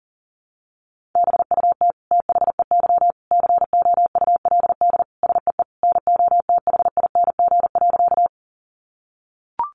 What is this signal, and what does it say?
Morse code, heard as a keyed tone, about 37 wpm, 700 Hz. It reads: BUT THEY COULD SEE NOTHING.
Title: Sentences from Top 500 Words 40wpm